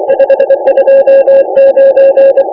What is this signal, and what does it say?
Morse code, heard as a keyed tone, about 24 wpm, 530 Hz. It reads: 529
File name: iw0ffk / 2009 07 03 JO6EDD IW0FFK rec by jo6edd